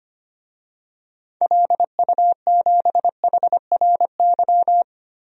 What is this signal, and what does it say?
Morse code, heard as a keyed tone, about 25 wpm, 700 Hz. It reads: LU7HRY
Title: DXCC-100 Call Signs - 6 Characters - Repeated 3x - 25wpm